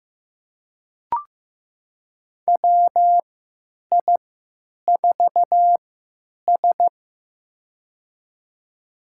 Morse code, heard as a keyed tone, about 15 wpm, spaced as for 9 wpm, 700 Hz. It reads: WI4S
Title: USA Call Signs - 4 Characters - 3x Character Spacing - 15wpm